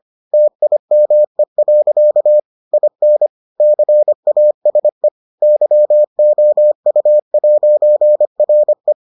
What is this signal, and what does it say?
Morse code, heard as a keyed tone, about 25 wpm, 600 Hz. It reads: TIME. IN CASE YOU'RE